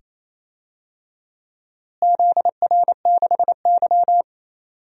Morse code, heard as a keyed tone, about 28 wpm, 700 Hz. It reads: ZR6Y
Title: DXCC-100 Call Signs - 4 Characters - Repeated 2x - 28wpm